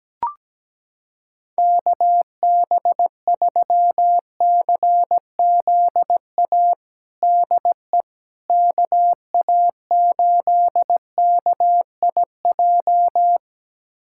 Morse code, heard as a keyed tone, about 17 wpm, 700 Hz. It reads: KB3CZA DE KA8KIJ
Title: USA Callsign DE Callsign - 6 Characters - 17wpm